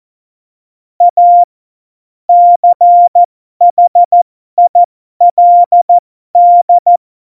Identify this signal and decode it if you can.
Morse code, heard as a keyed tone, about 14 wpm, spaced as for 12 wpm, 700 Hz. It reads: A CHILD
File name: Morsekurs / Witze